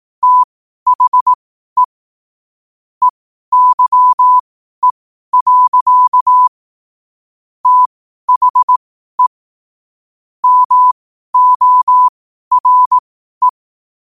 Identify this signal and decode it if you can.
Morse code, heard as a keyed tone, about 18 wpm, spaced as for 12 wpm, 1000 Hz. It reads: THE EYE. THE MORE
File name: qotdfiles / 12wpm